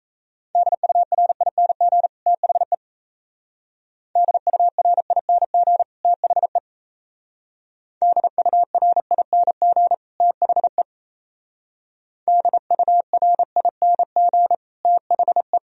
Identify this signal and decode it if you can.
Morse code, heard as a keyed tone, about 34 wpm, 700 Hz. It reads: DURING THE DURING THE DURING THE DURING THE
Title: Speed-Racing - Sets of 2 Words 28wpm